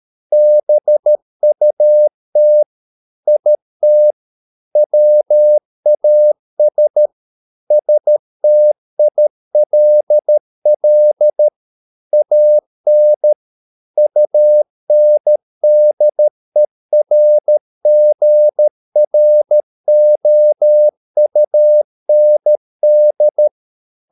Morse code, heard as a keyed tone, about 13 wpm, 600 Hz. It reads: BUT IT WAS STILL AN UNDERGROUND